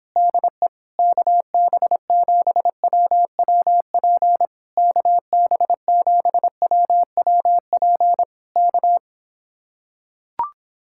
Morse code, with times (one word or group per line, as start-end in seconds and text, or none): 0.2-0.7 DE
1.0-4.5 KB7WWP
4.8-8.2 KB7WWP
8.6-9.0 K